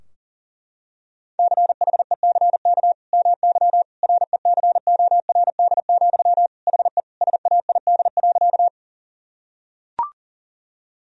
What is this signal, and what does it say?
Morse code, heard as a keyed tone, about 40 wpm, 700 Hz. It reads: CHECK MY RECORD, HE SAID.
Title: Sentences from Top 1000 Words - Review Part 3 - 40wpm